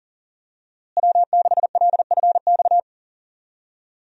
Morse code, heard as a keyed tone, about 40 wpm, 700 Hz. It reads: W6LFX
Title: USA Call Signs - 5 Characters - Repeated 2x - 40wpm